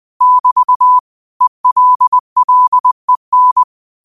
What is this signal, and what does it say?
Morse code, heard as a keyed tone, about 20 wpm, 1000 Hz. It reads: <BT> ELLEN